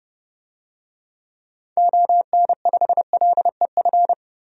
Morse code, heard as a keyed tone, about 30 wpm, 700 Hz. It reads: ON5LEF
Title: DXCC-100 Call Signs - 6 Characters - 30wpm